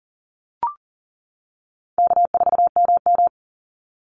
Morse code, heard as a keyed tone, about 38 wpm, 700 Hz. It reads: K4MM
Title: Easy Call Signs - 40wpm